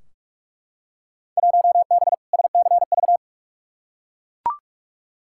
Morse code, heard as a keyed tone, about 43 wpm, 700 Hz. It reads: 1B SCV
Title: Field Day Exchanges - 45wpm